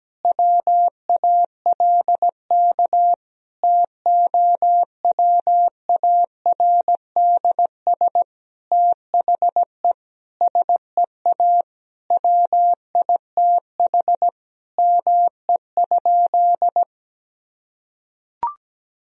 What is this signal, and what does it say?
Morse code, heard as a keyed tone, about 17 wpm, 700 Hz. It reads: WALK TOWARDS THE SEA WITH ME?
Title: Sentences from Top 1000 Words - Review Part 2 - 17wpm